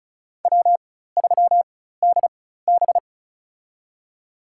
Morse code, heard as a keyed tone, about 36 wpm, spaced as for 16 wpm, 700 Hz. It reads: W3DB
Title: USA Call Signs - 4 Characters - 4x Character Spacing - 35wpm